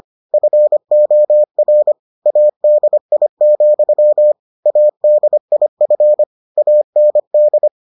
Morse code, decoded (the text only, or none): FOR ADI, ADIF AND